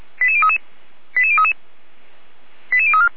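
Five-tone selective call (ZVEI), heard as quick runs of five tones, tones of about 70 ms. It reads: 88033 88033 88034
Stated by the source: ZVEI five-tone selective call 89305 received with an SDR, Feuerwehr Orlamünde.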